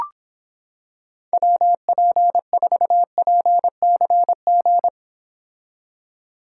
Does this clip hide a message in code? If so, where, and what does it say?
Yes: Morse code from 1.3 to 4.9 s, heard as a keyed tone, about 26 wpm, 700 Hz. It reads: WP4PCG